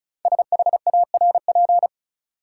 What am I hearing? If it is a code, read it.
Morse code, heard as a keyed tone, about 36 wpm, 700 Hz. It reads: SHARP
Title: Sentences from Top 1600 Words - New Word Drill Alt - 35wpm